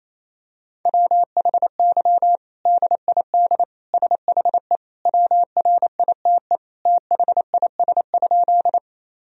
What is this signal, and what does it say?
Morse code, heard as a keyed tone, about 28 wpm, 700 Hz. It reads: WHY DID SHE WRITE THIS?
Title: Sentences from Top 1000 Words - Review Part 1 - 28wpm